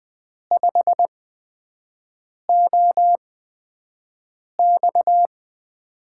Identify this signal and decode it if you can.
Morse code, heard as a keyed tone, about 20 wpm, spaced as for 5 wpm, 700 Hz. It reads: 5OX
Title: USA Call Signs - 8x Character Spacing - 20wpm